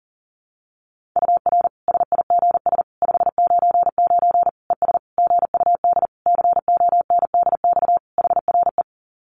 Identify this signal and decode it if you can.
Morse code, heard as a keyed tone, about 37 wpm, 700 Hz. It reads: UR SIGS 599 ES GUD CONDX HRE
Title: Common QSO Phrases 40wpm